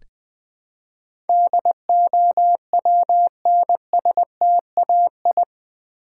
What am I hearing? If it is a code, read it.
Morse code, heard as a keyed tone, about 20 wpm, 700 Hz. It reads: DOWNSTAI
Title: Build A Word - 4 Letters At A Time - 10-Letter Words - 20wpm